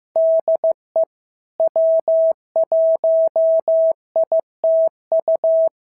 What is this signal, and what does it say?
Morse code, heard as a keyed tone, about 15 wpm, 650 Hz. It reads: DE W1ITU
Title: SOTA QSO Practice 15wpm